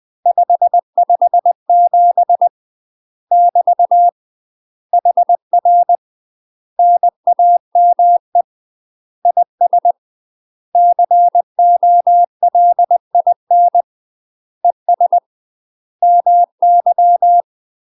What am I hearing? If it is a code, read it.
Morse code, heard as a keyed tone, about 20 wpm, 700 Hz. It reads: 557 <BT> HR NAME IS COLIN ES MY
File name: UK MASTER QSOs TRAINING / 20WPM / UK MASTER QSOs 0046